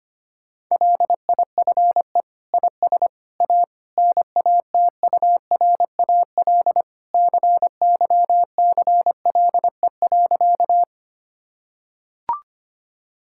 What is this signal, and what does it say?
Morse code, heard as a keyed tone, about 25 wpm, 700 Hz. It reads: LIFE IS A NATURAL CYCLE.
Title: Sentences from Top 1000 Words - Review Part 6 - 25wpm